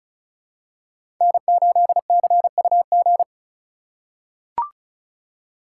Morse code, heard as a keyed tone, about 36 wpm, 700 Hz. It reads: N8CUG
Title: USA Call Signs - 4x Character Spacing - 35wpm